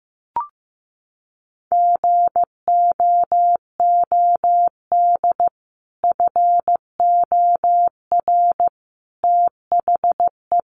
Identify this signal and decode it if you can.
Morse code, heard as a keyed tone, about 15 wpm, 700 Hz. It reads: GOOD FOR THE
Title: Sentences from Top 300 Words 15wpm